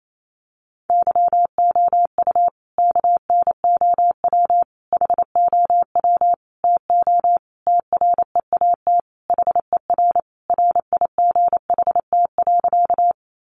Morse code, heard as a keyed tone, about 28 wpm, 700 Hz. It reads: YOU KNOW HOW TO TREAT HER RIGHT.
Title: Sentences from Top 1300 Words 28wpm